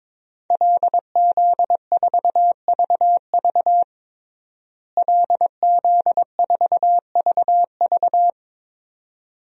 Morse code, heard as a keyed tone, about 22 wpm, 700 Hz. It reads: LZ4VV LZ4VV
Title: DXCC-100 Call Signs - 5 Characters - Repeated 2x - 22wpm